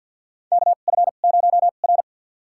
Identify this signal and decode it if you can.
Morse code, heard as a keyed tone, about 49 wpm, spaced as for 36 wpm, 700 Hz. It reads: KF0R
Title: USA Call Signs - 4 Characters - 2x Character Spacing - 50wpm